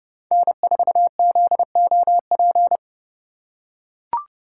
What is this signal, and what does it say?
Morse code, heard as a keyed tone, about 32 wpm, 700 Hz. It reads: N4ZOP
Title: USA Call Signs - 5 Characters - 4x Character Spacing - 30wpm